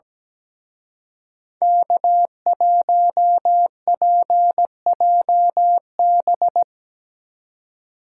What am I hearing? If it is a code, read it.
Morse code, heard as a keyed tone, about 19 wpm, 700 Hz. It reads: K1PJB
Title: Speed Racing - USA 1x3 Call Signs 17wpm